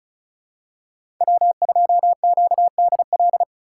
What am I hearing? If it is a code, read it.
Morse code, heard as a keyed tone, about 34 wpm, 700 Hz. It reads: W2QDL
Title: USA Call Signs - 3x Character Spacing - 35wpm